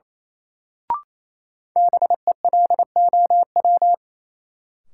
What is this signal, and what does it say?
Morse code, heard as a keyed tone, about 28 wpm, 700 Hz. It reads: BELOW
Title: Sentences from Top 1000 Words - Review Part 5 - New Word Drill - 28wpm